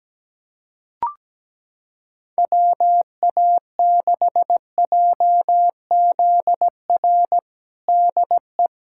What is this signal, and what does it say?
Morse code, heard as a keyed tone, about 17 wpm, 700 Hz. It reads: WA6JZR DE
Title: USA Callsign DE Callsign - 6 Characters - 17wpm